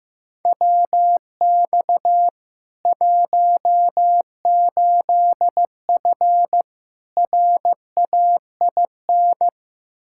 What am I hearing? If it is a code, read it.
Morse code, heard as a keyed tone, about 15 wpm, 700 Hz. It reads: WX 18F RAIN
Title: Practice QSO - Exchange 2 - Info - 15wpm